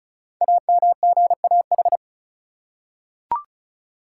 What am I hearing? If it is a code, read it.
Morse code, heard as a keyed tone, about 36 wpm, 690 Hz. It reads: AMGAH
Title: Code Groups - 5 Letters - 35wpm